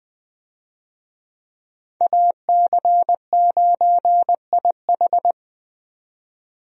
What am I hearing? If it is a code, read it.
Morse code, heard as a keyed tone, about 20 wpm, 700 Hz. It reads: AC9IH
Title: USA Call Signs - 5 Characters - 20wpm